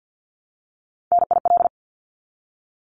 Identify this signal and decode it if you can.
Morse code, heard as a keyed tone, about 50 wpm, 700 Hz. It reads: NIL
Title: Abbreviations Repeated 3x - 50wpm